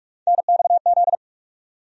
Morse code, heard as a keyed tone, about 46 wpm, 700 Hz. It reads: NXZ
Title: USA Call Sign Postfixes - 3 Letters 45wpm